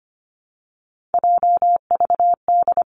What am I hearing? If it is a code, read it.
Morse code, heard as a keyed tone, about 25 wpm, 700 Hz. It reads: JVD
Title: Three Challenging Letter-Number Combinations 25wpm